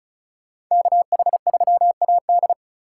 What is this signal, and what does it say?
Morse code, heard as a keyed tone, about 45 wpm, 700 Hz. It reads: KH3AD